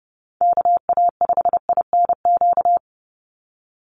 Morse code, heard as a keyed tone, about 30 wpm, 700 Hz. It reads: KA5INQ
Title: USA Call Signs 30wpm